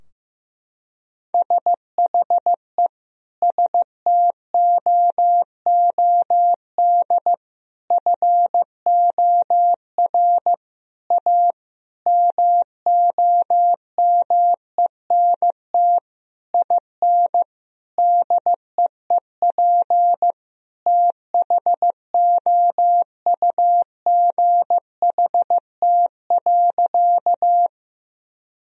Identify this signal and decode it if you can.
Morse code, heard as a keyed tone, about 15 wpm, 700 Hz. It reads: SHE STOOD FOR A MOMENT IN DEEP THOUGHT.
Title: Sentences from Top 1000 Words - Review Part 3 - 15wpm